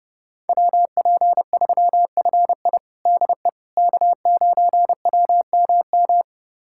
Morse code, heard as a keyed tone, about 32 wpm, 700 Hz. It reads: WP3FI DE K9WMM
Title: USA Callsign DE Callsign - 5 Characters - 30wpm